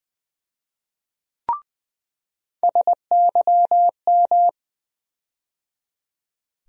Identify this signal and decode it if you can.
Morse code, heard as a keyed tone, about 20 wpm, 700 Hz. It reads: SYM
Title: USA Call Sign Postfixes - 3 Letters 20wpm